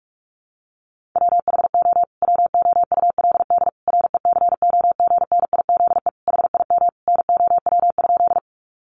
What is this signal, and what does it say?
Morse code, heard as a keyed tone, about 44 wpm, 700 Hz. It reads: WHO WOULD RECOGNIZE HIM NOW?
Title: Sentences from Top 1400 Words 45wpm